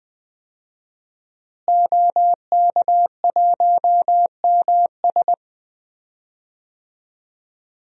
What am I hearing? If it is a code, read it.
Morse code, heard as a keyed tone, about 20 wpm, 700 Hz. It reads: OK1MS
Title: DXCC-100 Call Signs - 5 Characters - Repeated 2x - 20wpm